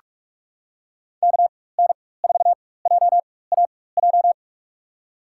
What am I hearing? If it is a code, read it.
Morse code, heard as a keyed tone, about 46 wpm, spaced as for 21 wpm, 700 Hz. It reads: KN4JAJ
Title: USA Call Signs - 6 Characters - 4x Character Spacing - 45wpm